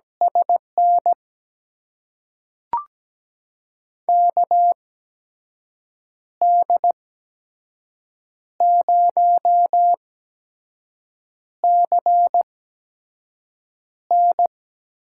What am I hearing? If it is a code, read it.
Morse code, heard as a keyed tone, about 17 wpm, spaced as for 5 wpm, 700 Hz. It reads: SN KD0CN
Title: USA Call Signs - 6 Characters - 8x Character Spacing - 17wpm